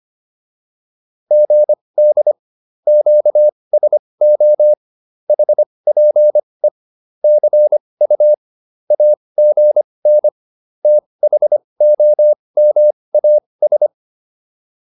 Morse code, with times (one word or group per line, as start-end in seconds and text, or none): 1.3-2.3 GD
2.9-4.7 QSO
5.3-6.7 HPE
7.2-8.3 CU
8.9-10.3 AGN
10.8-13.9 THOMAS